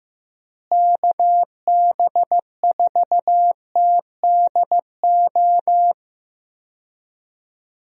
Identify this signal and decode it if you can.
Morse code, heard as a keyed tone, about 15 wpm, 700 Hz. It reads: KB4TDO
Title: USA Call Signs - 6 Characters - Repeated 3x - 15wpm